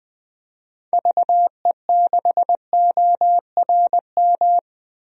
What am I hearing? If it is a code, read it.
Morse code, heard as a keyed tone, about 20 wpm, 700 Hz. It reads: VE6ORM
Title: DXCC-100 Call Signs - 6 Characters - 20wpm